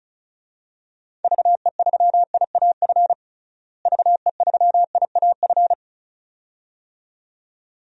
Morse code, heard as a keyed tone, about 34 wpm, 700 Hz. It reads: VE3IAF VE3IAF
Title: DXCC-100 Call Signs - 6 Characters - Repeated 2x - 35wpm